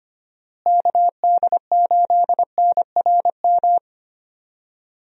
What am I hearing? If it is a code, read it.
Morse code, heard as a keyed tone, about 25 wpm, 700 Hz. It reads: KD8NRM
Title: USA Call Signs - 6 Characters - Repeated 2x - 25wpm